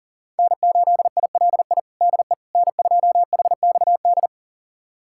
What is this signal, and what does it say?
Morse code, heard as a keyed tone, about 40 wpm, 700 Hz. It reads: N8ILI DE N2HXD
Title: USA Callsign DE Callsign - 5 Characters - 40wpm